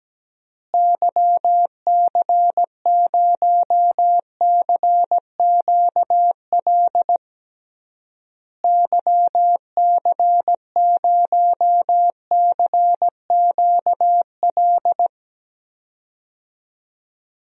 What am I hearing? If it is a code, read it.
Morse code, heard as a keyed tone, about 17 wpm, 700 Hz. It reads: YC0CQL YC0CQL
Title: DXCC-100 Call Signs - 6 Characters - Repeated 3x - 17wpm